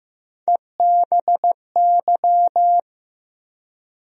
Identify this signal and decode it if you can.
Morse code, heard as a keyed tone, about 15 wpm, 700 Hz. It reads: EBY
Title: USA Call Sign Postfixes - 3 Letters 15wpm